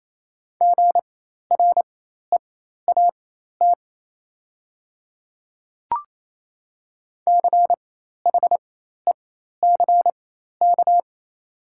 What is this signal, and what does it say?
Morse code, heard as a keyed tone, about 28 wpm, spaced as for 13 wpm, 700 Hz. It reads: GREAT CHECK
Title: 4x Character Spacing - Top 500 Words - 28wpm